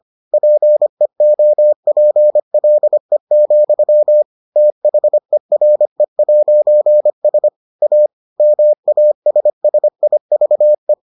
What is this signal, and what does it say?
Morse code, heard as a keyed tone, about 25 wpm, 600 Hz. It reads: PEOPLE, THERE'S A MASSIVE